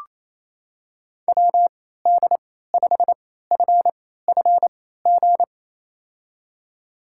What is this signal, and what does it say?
Morse code, heard as a keyed tone, about 28 wpm, spaced as for 16 wpm, 700 Hz. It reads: WD5FFG
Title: USA Call Signs - 6 Characters - 3x Character Spacing - 28wpm